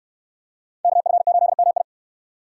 Morse code, heard as a keyed tone, about 62 wpm, 700 Hz. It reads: KF8MI